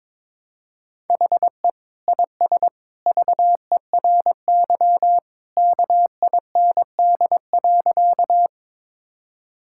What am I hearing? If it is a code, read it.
Morse code, heard as a keyed tone, about 22 wpm, 700 Hz. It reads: HE IS VERY KIND.
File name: practice / 22wpm / Sentences with 2-Word Phrases - 22wpm